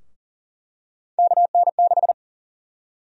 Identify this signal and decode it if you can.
Morse code, heard as a keyed tone, about 40 wpm, 700 Hz. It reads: KN6
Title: USA Call Sign Prefixes - Repeated 2x - 40wpm